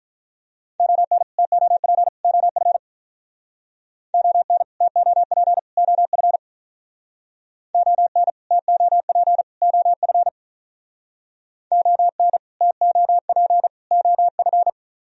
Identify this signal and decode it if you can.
Morse code, heard as a keyed tone, about 44 wpm, 700 Hz. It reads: ON TOP OF ON TOP OF ON TOP OF ON TOP OF
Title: Speed-Racing - Sets of 3 Words 35wpm